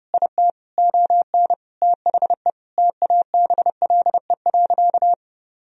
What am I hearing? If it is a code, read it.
Morse code, heard as a keyed tone, about 32 wpm, 700 Hz. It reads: IT ON THE TABLE.
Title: Sentences from Top 1000 Words - Review Part 2 - 30wpm